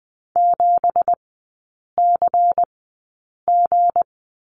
Morse code, heard as a keyed tone, about 20 wpm, spaced as for 8 wpm, 700 Hz. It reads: 7CG
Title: 184 Lesson 30e - Letter b - Three Characters 20wpm